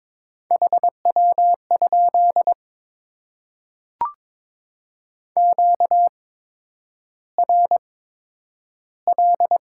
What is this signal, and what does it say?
Morse code, heard as a keyed tone, about 22 wpm, spaced as for 9 wpm, 700 Hz. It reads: HW? QRL
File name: Top 20 QSO Elements - 8x Character Spacing - 22wpm